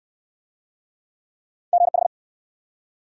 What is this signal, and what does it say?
Morse code, heard as a keyed tone, about 60 wpm, 700 Hz. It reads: GH